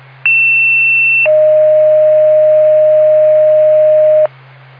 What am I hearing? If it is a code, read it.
Two-tone page, heard as a one-second tone, then a three-second tone, about 2700 then 620 Hz. It is heard from 0.2 to 4.3 s.